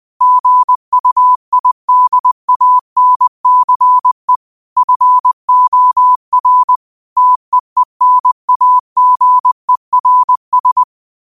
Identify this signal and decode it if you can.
Morse code, heard as a keyed tone, about 20 wpm, 1000 Hz. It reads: GUIDANCE FOR TEENAGERS